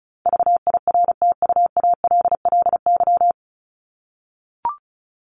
Morse code, heard as a keyed tone, about 34 wpm, 700 Hz. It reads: VIRTUALLY
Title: Top 1501-1750 Words 35wpm